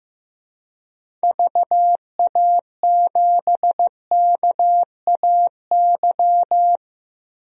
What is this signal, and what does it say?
Morse code, heard as a keyed tone, about 15 wpm, 700 Hz. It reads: VA7KAY
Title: DXCC-100 Call Signs - 6 Characters - Repeated 2x - 15wpm